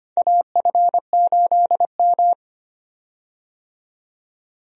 Morse code, heard as a keyed tone, about 25 wpm, 700 Hz. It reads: AF8M
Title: USA Call Signs - 4 Characters - 25wpm